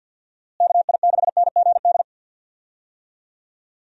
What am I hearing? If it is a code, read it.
Morse code, heard as a keyed tone, about 52 wpm, 700 Hz. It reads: KI6NGD
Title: USA Call Signs - 6 Characters - Repeated 3x - 50wpm